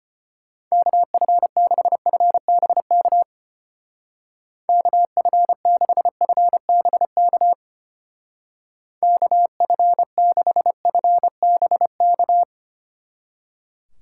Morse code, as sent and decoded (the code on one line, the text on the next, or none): -.- ..-. -.... ..-. -... -.- / -.- ..-. -.... ..-. -... -.- / -.- ..-. -.... ..-. -... -.-
KF6FBK KF6FBK KF6FBK